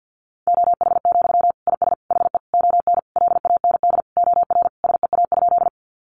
Morse code, heard as a keyed tone, about 50 wpm, 700 Hz. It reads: OH, IS HE ON LAND OR SEA?
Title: Sentences from Top 500 Words 50wpm